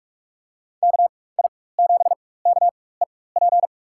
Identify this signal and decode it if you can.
Morse code, heard as a keyed tone, about 42 wpm, spaced as for 20 wpm, 700 Hz. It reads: KI7KEP
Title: USA Call Signs - 4x Character Spacing - 45wpm